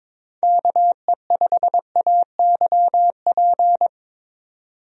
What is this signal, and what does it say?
Morse code, heard as a keyed tone, about 23 wpm, 700 Hz. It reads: KE5AYP
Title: USA Call Signs - 6 Characters - 22wpm